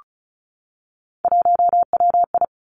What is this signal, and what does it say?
Morse code, heard as a keyed tone, about 36 wpm, 700 Hz. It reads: 1WI
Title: Three Challenging Letter-Number Combinations 35wpm